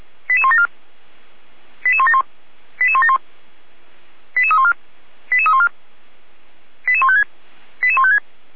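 Five-tone selective call (ZVEI), heard as quick runs of five tones, tones of about 70 ms. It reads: 88174 88171 88171 88314 88314 88156 88156
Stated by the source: ZVEI five-tone selective call received on SDR, archived by fire brigade Feuerwehr Orlamünde.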